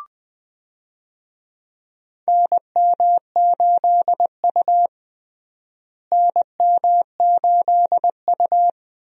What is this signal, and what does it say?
Morse code, heard as a keyed tone, about 20 wpm, 700 Hz. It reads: NM8U NM8U
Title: USA Call Signs - 4 Characters - Repeated 2x - 20wpm